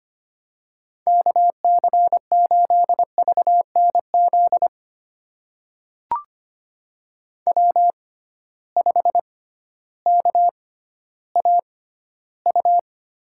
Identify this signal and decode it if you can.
Morse code, heard as a keyed tone, about 25 wpm, spaced as for 13 wpm, 700 Hz. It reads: KC8VNZ W5KAU
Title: USA Call Signs - 6x Character Spacing - 25wpm